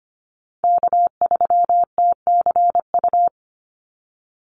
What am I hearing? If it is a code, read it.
Morse code, heard as a keyed tone, about 25 wpm, 700 Hz. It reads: K3TCU
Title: NA SOTA Chaser Callsigns 25wpm